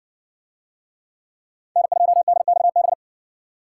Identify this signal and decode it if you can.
Morse code, heard as a keyed tone, about 57 wpm, 700 Hz. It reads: N2DXB